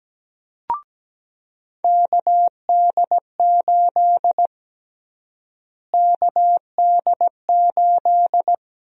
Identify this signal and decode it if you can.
Morse code, heard as a keyed tone, about 17 wpm, 700 Hz. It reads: KD8 KD8
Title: USA Call Sign Prefixes - Repeated 2x - 17wpm